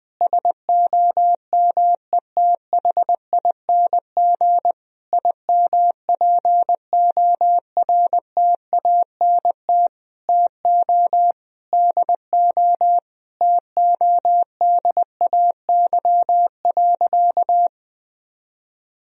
Morse code, heard as a keyed tone, about 21 wpm, 700 Hz. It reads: SOMETHING IMPORTANT TO DO TODAY.
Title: Sentences from Top 1000 Words - Review Part 4 - 20wpm